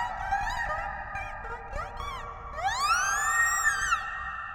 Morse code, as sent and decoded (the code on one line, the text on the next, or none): none